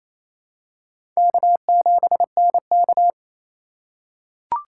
Morse code, heard as a keyed tone, about 28 wpm, 700 Hz. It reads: K7NK
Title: USA Call Signs - 4 Characters - 2x Character Spacing - 28wpm